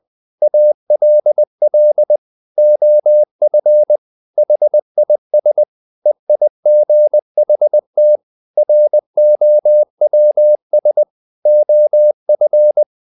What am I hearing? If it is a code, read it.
Morse code, heard as a keyed tone, about 20 wpm, 600 Hz. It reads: ALL OF HIS EIGHT ROWS OF
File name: the old man and the sea part 2 20wpm